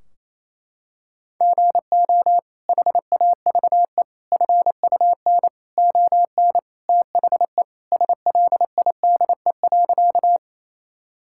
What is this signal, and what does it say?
Morse code, heard as a keyed tone, about 28 wpm, 700 Hz. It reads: GO HAVE FUN ON THE SLIDE.
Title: Sentences from Top 1600 Words 28wpm